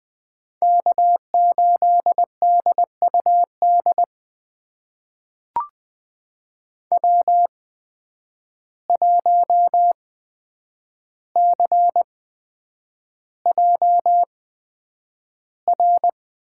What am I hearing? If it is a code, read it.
Morse code, heard as a keyed tone, about 20 wpm, spaced as for 9 wpm, 700 Hz. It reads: K8DUD W1CJR
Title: USA Call Signs - 5 Characters - 8x Character Spacing - 20wpm